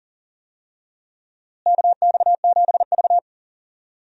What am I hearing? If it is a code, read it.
Morse code, heard as a keyed tone, about 40 wpm, 700 Hz. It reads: KX7V
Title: USA Call Signs - 4 Characters - Repeated 3x - 40wpm